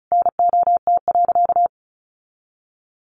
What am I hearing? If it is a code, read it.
Morse code, heard as a keyed tone, about 34 wpm, 700 Hz. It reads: NOT.
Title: Sentences from Top 900 Words 35wpm